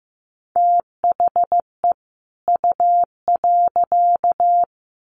Morse code, heard as a keyed tone, about 15 wpm, 700 Hz. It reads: THE U.
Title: Sets of 2 Words 15wpm